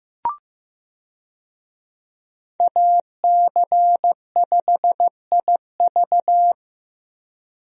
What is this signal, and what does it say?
Morse code, heard as a keyed tone, about 16 wpm, 700 Hz. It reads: AC5IV
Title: USA Call Signs - 5 Characters - 15wpm